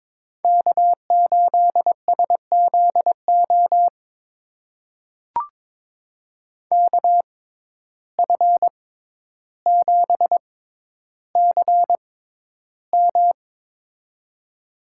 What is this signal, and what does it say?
Morse code, heard as a keyed tone, about 21 wpm, spaced as for 11 wpm, 700 Hz. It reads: K8SZO KF7CM
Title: USA Call Signs - 5 Characters - 6x Character Spacing - 22wpm